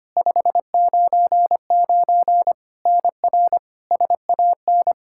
Morse code, heard as a keyed tone, about 25 wpm, 700 Hz. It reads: 599 NR SAN
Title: K1USN SST QSO Practice 25wpm